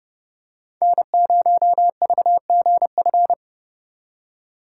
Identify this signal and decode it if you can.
Morse code, heard as a keyed tone, about 30 wpm, 700 Hz. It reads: N0VGF
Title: USA Call Signs - 5 Characters - 3x Character Spacing - 30wpm